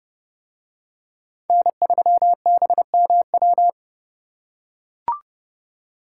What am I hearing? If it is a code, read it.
Morse code, heard as a keyed tone, about 31 wpm, 700 Hz. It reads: N3BMW